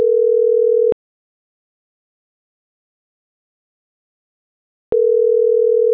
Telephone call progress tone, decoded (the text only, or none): ringback tone